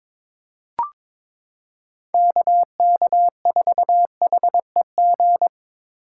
Morse code, heard as a keyed tone, about 22 wpm, 700 Hz. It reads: KK4HEG
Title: K1USN SST QSO Practice - 22wpm